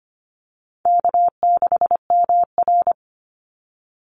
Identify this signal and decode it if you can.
Morse code, heard as a keyed tone, about 25 wpm, 700 Hz. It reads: K6MR
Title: Call Signs - 25wpm